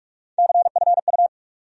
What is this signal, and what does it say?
Morse code, heard as a keyed tone, about 42 wpm, 700 Hz. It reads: CFU
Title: USA Call Sign Postfixes - 3 Letters 45wpm